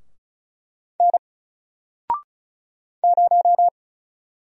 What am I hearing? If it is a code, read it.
Morse code, heard as a keyed tone, about 36 wpm, 700 Hz. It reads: N 0
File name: Single-Letter-Number-35wpm